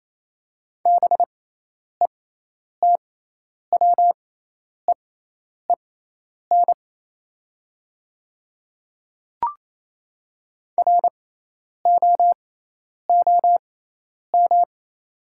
Morse code, heard as a keyed tone, about 28 wpm, spaced as for 10 wpm, 700 Hz. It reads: BETWEEN ROOM
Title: 6x Character Spacing - Top 300 Words - 28wpm